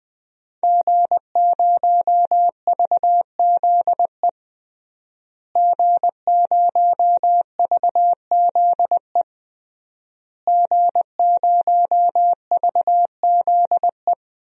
Morse code, heard as a keyed tone, about 20 wpm, 700 Hz. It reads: G0VZE G0VZE G0VZE
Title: DXCC-100 Call Signs - 5 Characters - Repeated 3x - 20wpm